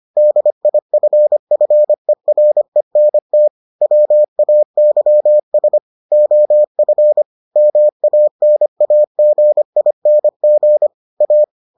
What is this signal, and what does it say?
Morse code, heard as a keyed tone, about 25 wpm, 600 Hz. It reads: DIFFERENT WAYS OF MANAGING A